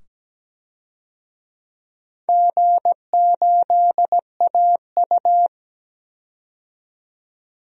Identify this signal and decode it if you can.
Morse code, heard as a keyed tone, about 17 wpm, 700 Hz. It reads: G8AU
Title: DXCC-100 Call Signs - 4 Characters - 17wpm